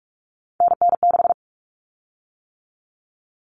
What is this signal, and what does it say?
Morse code, heard as a keyed tone, about 44 wpm, 700 Hz. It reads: NN6
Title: Common Cut Numbers - 3-Digit 45wpm